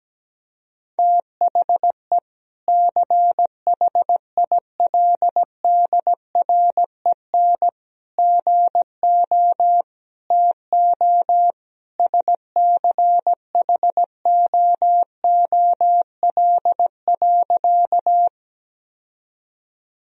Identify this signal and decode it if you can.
Morse code, heard as a keyed tone, about 17 wpm, 700 Hz. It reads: THE CHILDREN GO TO SCHOOL.
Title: Sentences from Top 1000 Words - Review Part 1 - 17wpm